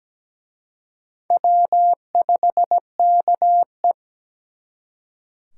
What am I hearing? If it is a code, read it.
Morse code, heard as a keyed tone, about 20 wpm, 700 Hz. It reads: W5KE
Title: Speed Racing - USA 1x2 Call Signs 17wpm